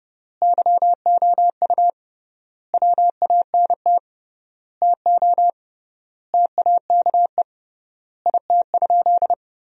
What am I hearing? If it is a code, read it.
Morse code, heard as a keyed tone, about 32 wpm, 700 Hz. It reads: YOU WANT TO TAKE IT?
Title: 3x Word-Spacing - Sentences from Top 300 Words - 30wpm